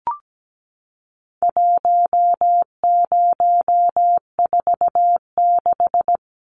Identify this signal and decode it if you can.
Morse code, heard as a keyed tone, about 17 wpm, 700 Hz. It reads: 1046
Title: Numbers - 4-Digit 17wpm